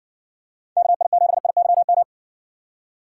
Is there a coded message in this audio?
Yes: Morse code from 0.8 to 2.0 s, heard as a keyed tone, about 60 wpm, 700 Hz. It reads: KI7IYM